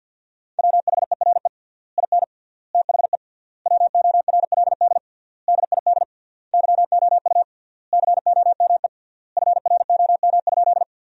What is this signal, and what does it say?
Morse code, heard as a keyed tone, about 50 wpm, 700 Hz. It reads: WHERE IN THE WORLD DID YOU COME FROM?
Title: 3x Word-Spacing - Sentences from Top 200 Words - 50wpm